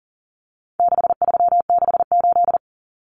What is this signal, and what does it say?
Morse code, heard as a keyed tone, about 40 wpm, 700 Hz. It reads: 6368
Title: Common Cut Numbers - 4-Digit 40wpm